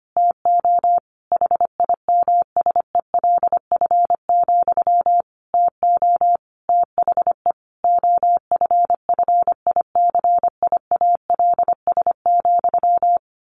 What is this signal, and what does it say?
Morse code, heard as a keyed tone, about 25 wpm, 700 Hz. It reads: TO HIMSELF, TO THE OFFICIALS,
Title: A Sound Of Thunder 25wpm